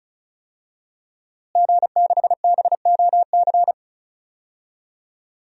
Morse code, heard as a keyed tone, about 34 wpm, 700 Hz. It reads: G6BOC
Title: DXCC-100 Call Signs - 5 Characters - Repeated 3x - 35wpm